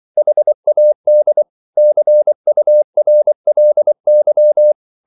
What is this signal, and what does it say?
Morse code, heard as a keyed tone, about 24 wpm, 600 Hz. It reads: HAD CURLY